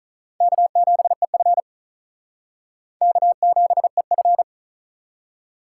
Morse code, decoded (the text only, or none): K7EF K7EF